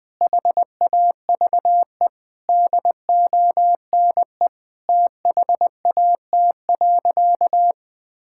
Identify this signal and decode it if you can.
Morse code, heard as a keyed tone, about 20 wpm, 700 Hz. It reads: HAVE DONE THAT.
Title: Sentences from Top 1600 Words 20wpm